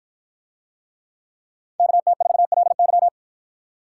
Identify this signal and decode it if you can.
Morse code, heard as a keyed tone, about 53 wpm, 700 Hz. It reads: KT4LY